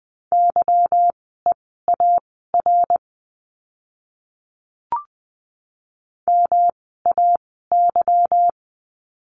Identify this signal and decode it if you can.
Morse code, heard as a keyed tone, about 20 wpm, spaced as for 14 wpm, 700 Hz. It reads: YEAR MAY
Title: 244 Lesson 39f - Letter z - Words 20wpm